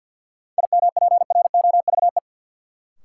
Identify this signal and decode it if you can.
Morse code, heard as a keyed tone, about 48 wpm, 700 Hz. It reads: IMPROVE
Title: Sentences from Top 800 Words - New Word Drill - 50wpm